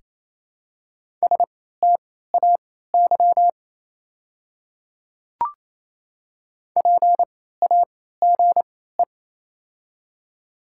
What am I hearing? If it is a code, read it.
Morse code, heard as a keyed tone, about 28 wpm, spaced as for 16 wpm, 700 Hz. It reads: STAY PAGE
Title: 3x Character Spacing - Top 500 Words - 28wpm